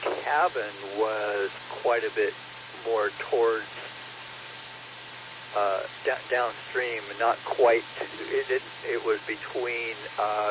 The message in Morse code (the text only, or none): none